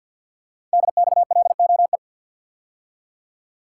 Morse code, heard as a keyed tone, about 48 wpm, 700 Hz. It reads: DXLOE
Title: Code Groups - 5 Letters - 50wpm